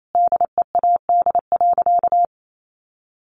Morse code, heard as a keyed tone, about 28 wpm, 700 Hz. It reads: DEAD.
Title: Sentences from Top 1500 Words 28wpm